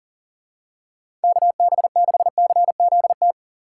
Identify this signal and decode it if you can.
Morse code, heard as a keyed tone, about 40 wpm, 700 Hz. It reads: KB6CZT
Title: USA Call Signs - 2x Character Spacing - 40wpm